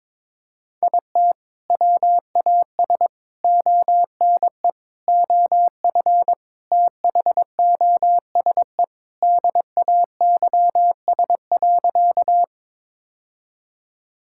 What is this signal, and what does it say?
Morse code, heard as a keyed tone, about 22 wpm, 700 Hz. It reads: IT WAS ONE OF THOSE DAYS.
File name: practice / 22wpm / Sentences with 3-Word Phrases - 22wpm